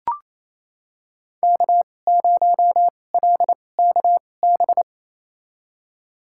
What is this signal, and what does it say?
Morse code, heard as a keyed tone, about 28 wpm, spaced as for 20 wpm, 700 Hz. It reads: K0LKB
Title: USA Call Signs - 5 Characters - 2x Character Spacing - 28wpm